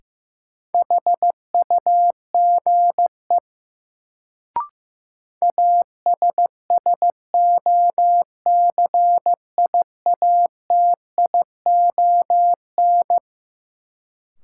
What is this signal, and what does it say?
Morse code, heard as a keyed tone, about 15 wpm, 700 Hz. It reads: HUGE ASSOCIATION
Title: Sentences from Top 1100 Words - New Word Drill Alt - 15wpm